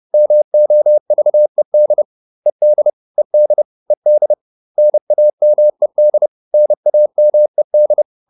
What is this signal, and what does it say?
Morse code, heard as a keyed tone, about 30 wpm, 600 Hz. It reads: MOVED ED ED ED NAMED NAMED